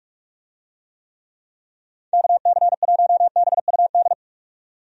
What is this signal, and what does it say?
Morse code, heard as a keyed tone, about 43 wpm, 700 Hz. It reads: KC1BUD
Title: USA Call Signs - 6 Characters - 45wpm